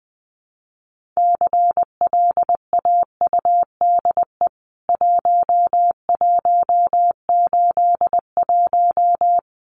Morse code, heard as a keyed tone, about 20 wpm, 700 Hz. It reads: CLAUDE 1181
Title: CWT Exchange 20wpm